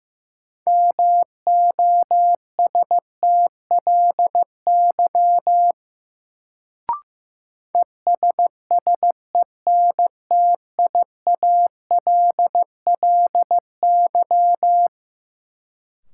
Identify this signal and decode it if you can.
Morse code, heard as a keyed tone, about 15 wpm, 700 Hz. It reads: MOSTLY ESSENTIALLY
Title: Sentences from Top 1300 Words - New Word Drill - 15wpm